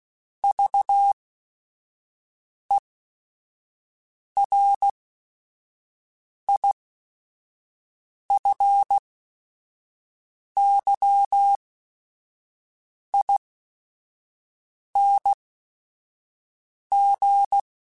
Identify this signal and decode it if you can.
Morse code, heard as a keyed tone, about 16 wpm, spaced as for 5 wpm, 800 Hz. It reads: VERIFYING